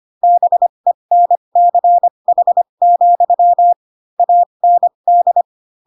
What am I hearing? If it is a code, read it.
Morse code, heard as a keyed tone, about 25 wpm, spaced as for 22 wpm, 700 Hz. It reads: BENCH, AND
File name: Pinocchio / Pinocchio ch4 18 25 ik1hge